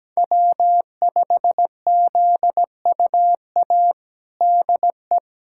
Morse code, heard as a keyed tone, about 17 wpm, 700 Hz. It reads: W5ZUA DE
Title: USA Callsign DE Callsign 17wpm